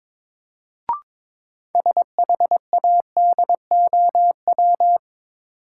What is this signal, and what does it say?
Morse code, heard as a keyed tone, about 22 wpm, 700 Hz. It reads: SHADOW
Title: Sets of 5 Words Encore - New Word Drill - 22wpm